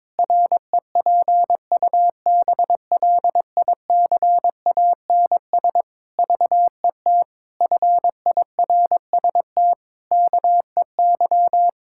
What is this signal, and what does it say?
Morse code, heard as a keyed tone, about 22 wpm, 700 Hz. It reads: REPUBLICANS VET FIRST KEY
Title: Pangram Challenge v2 - 22wpm